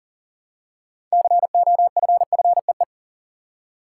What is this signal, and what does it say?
Morse code, heard as a keyed tone, about 41 wpm, 700 Hz. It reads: COFFEE
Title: Six Letter Words - 40wpm